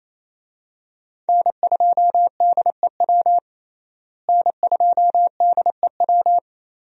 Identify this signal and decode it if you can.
Morse code, heard as a keyed tone, about 28 wpm, 700 Hz. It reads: N2DEW N2DEW
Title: USA Call Signs - 5 Characters - Repeated 3x - 28wpm